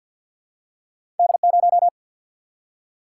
Morse code, heard as a keyed tone, about 54 wpm, 700 Hz. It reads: D0
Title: Single-Letter Single-Number 50wpm